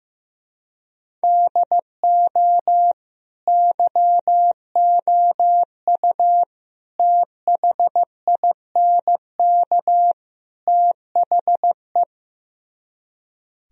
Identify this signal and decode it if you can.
Morse code, heard as a keyed tone, about 15 wpm, 700 Hz. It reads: DO YOU THINK THE
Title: Speed-Racing - Sets of 4 Words 15wpm